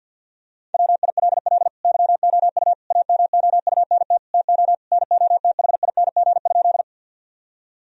Morse code, heard as a keyed tone, about 50 wpm, 700 Hz. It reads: WILL YOU AMOUNT TO NOTHING?
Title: Sentences from Top 1000 Words - Review Part 4 - 50wpm